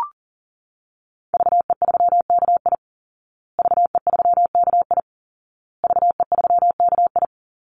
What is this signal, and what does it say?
Morse code, heard as a keyed tone, about 37 wpm, 700 Hz. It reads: VE3KI VE3KI VE3KI
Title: Call Signs Repeated 2x - 40wpm